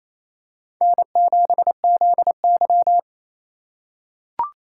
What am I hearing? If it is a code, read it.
Morse code, heard as a keyed tone, about 28 wpm, 700 Hz. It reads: N7ZY
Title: USA Call Signs - 4 Characters - 4x Character Spacing - 28wpm